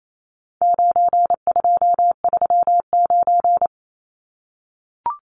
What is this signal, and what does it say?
Morse code, heard as a keyed tone, about 28 wpm, 700 Hz. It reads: 9239
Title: Numbers - 4-Digit 28wpm